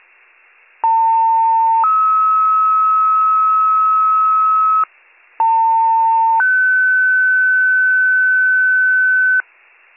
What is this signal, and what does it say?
Two-tone page, heard as a one-second tone, then a three-second tone, about 900 then 1290 Hz, about 900 then 1500 Hz, with noise. It is heard from 0.8 to 9.4 s.